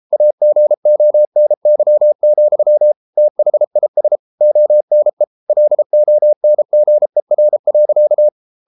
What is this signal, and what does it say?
Morse code, heard as a keyed tone, about 33 wpm, 600 Hz. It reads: AGONY, THIS ONE LONGER.